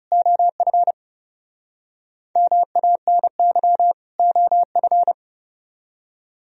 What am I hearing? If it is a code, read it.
Morse code, heard as a keyed tone, about 37 wpm, 700 Hz. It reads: OF MANY OF
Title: Speed-Racing - Sets of 2 Words 30wpm